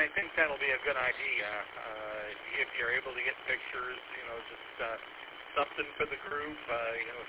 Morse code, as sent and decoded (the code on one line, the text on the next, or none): none